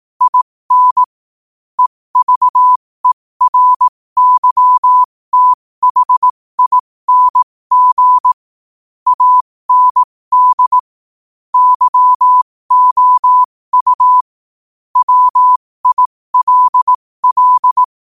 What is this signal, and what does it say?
Morse code, heard as a keyed tone, about 18 wpm, spaced as for 15 wpm, 1000 Hz. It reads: IN EVERYTHING AND YOU WILL
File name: 15wpm